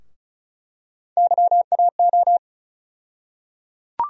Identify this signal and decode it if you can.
Morse code, heard as a keyed tone, about 36 wpm, 700 Hz. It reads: YAO
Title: USA Call Sign Postfixes - 3 Letters 35wpm